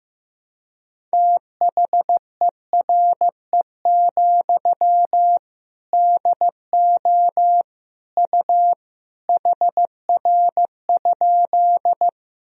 Morse code, heard as a keyed tone, about 15 wpm, 700 Hz. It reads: THERE, DO U HR?
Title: Sentences from Top 300 Words and Abbreviations 15wpm